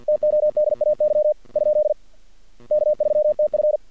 Morse code, heard as a keyed tone, about 48 wpm, 600 Hz. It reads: TODAY ? K3TF
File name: archived / CRSnet